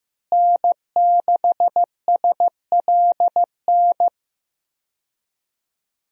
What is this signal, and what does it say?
Morse code, heard as a keyed tone, about 15 wpm, 700 Hz. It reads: N6SLN
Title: USA Call Signs - 5 Characters - Repeated 2x - 15wpm